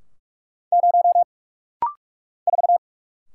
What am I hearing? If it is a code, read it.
Morse code, heard as a keyed tone, about 46 wpm, 700 Hz. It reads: 0 4
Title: Single Letter-Number - Rapid-Fire 45wpm